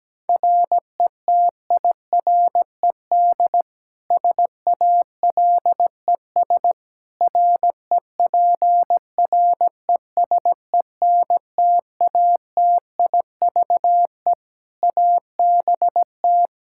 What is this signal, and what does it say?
Morse code, heard as a keyed tone, about 17 wpm, 700 Hz. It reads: RETIRED SALES REPRESENTATIVE ABT
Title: Practice QSO - Exchange 3 - More Info - 17wpm